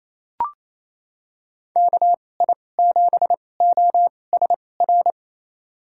Morse code, heard as a keyed tone, about 28 wpm, spaced as for 20 wpm, 700 Hz. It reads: KI7OSR
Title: USA Call Signs - 6 Characters - 2x Character Spacing - 28wpm